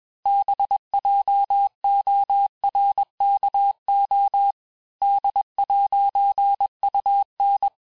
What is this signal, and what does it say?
Morse code, heard as a keyed tone, about 21 wpm, 800 Hz. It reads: BJORKO D'UN
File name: texte-clair-vitesse-15mpm-puis-20mpm